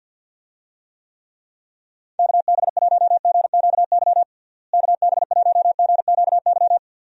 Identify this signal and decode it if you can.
Morse code, heard as a keyed tone, about 52 wpm, 700 Hz. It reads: KB1GQY KB1GQY
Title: USA Call Signs - 6 Characters - Repeated 3x - 50wpm